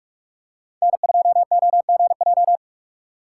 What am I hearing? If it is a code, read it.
Morse code, heard as a keyed tone, about 42 wpm, 700 Hz. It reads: N2OGJ
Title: USA Call Signs - 5 Characters - 6x Character Spacing - 45wpm